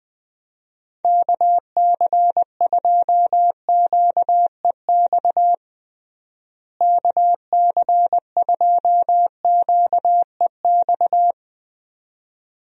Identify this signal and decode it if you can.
Morse code, heard as a keyed tone, about 20 wpm, 700 Hz. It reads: KC2QEX KC2QEX
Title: USA Call Signs - 6 Characters - Repeated 2x - 20wpm